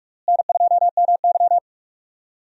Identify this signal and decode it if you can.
Morse code, heard as a keyed tone, about 46 wpm, 700 Hz. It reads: N2MY
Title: USA Call Signs - 4 Characters - Repeated 2x - 45wpm